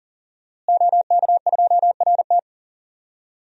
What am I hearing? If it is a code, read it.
Morse code, heard as a keyed tone, about 40 wpm, 700 Hz. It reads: OK2RT
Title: DXCC-100 Call Signs - 5 Characters - Repeated 3x - 40wpm